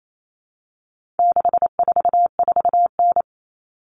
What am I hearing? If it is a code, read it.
Morse code, heard as a keyed tone, about 28 wpm, 700 Hz. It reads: 644N
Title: All Cut Numbers - 4-Digit 28wpm